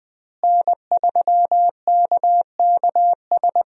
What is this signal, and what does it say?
Morse code, heard as a keyed tone, about 21 wpm, 700 Hz. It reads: N3KKS
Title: USA Call Signs - 5 Characters - Repeated 3x - 20wpm